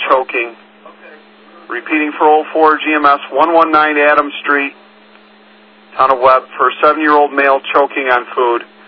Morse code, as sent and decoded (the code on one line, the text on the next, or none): none